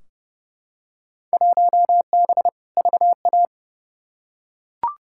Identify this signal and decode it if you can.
Morse code, heard as a keyed tone, about 30 wpm, 700 Hz. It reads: 1B VA
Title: Field Day Exchanges Encore - 30wpm